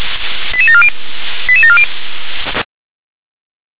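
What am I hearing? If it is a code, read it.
Five-tone selective call (ZVEI), heard as quick runs of five tones, tones of about 70 ms. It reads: 88740 88740
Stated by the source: ZVEI five-tone selective call received on an SDR.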